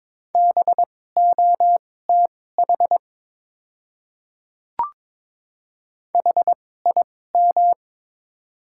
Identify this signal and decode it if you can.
Morse code, heard as a keyed tone, about 22 wpm, spaced as for 16 wpm, 700 Hz. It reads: BOTH HIM
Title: 2x Character Spacing - Top 300 Words - 22wpm